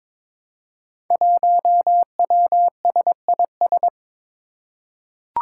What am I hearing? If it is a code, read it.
Morse code, heard as a keyed tone, about 23 wpm, 700 Hz. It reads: 1WSIS
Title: Five Challenging Letter-Number Combinations 22wpm